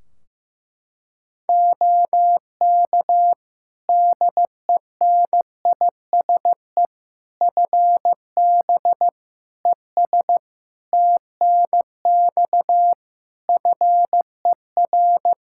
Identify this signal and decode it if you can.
Morse code, heard as a keyed tone, about 16 wpm, 700 Hz. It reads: OK DENISE FB ES TNX FER
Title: K1USN SST QSO Practice 15wpm